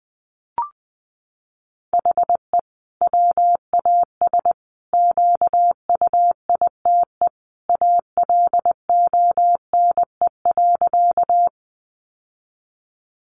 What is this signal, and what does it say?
Morse code, heard as a keyed tone, about 20 wpm, 700 Hz. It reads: HE WAS QUITE ALONE.
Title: Sentences from Top 1000 Words 20wpm